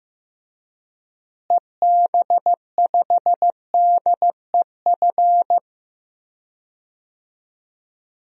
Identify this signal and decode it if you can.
Morse code, heard as a keyed tone, about 15 wpm, 700 Hz. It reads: EB5DEF
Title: DXCC-100 Call Signs - 6 Characters - Repeated 3x - 15wpm